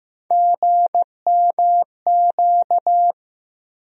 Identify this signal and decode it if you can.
Morse code, heard as a keyed tone, about 15 wpm, 700 Hz. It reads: GMQ
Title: USA Call Sign Postfixes - 3 Letters 15wpm